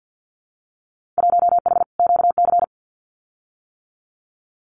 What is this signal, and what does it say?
Morse code, heard as a keyed tone, about 50 wpm, 700 Hz. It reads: 1H QC
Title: Winter Field Day Exchanges 50wpm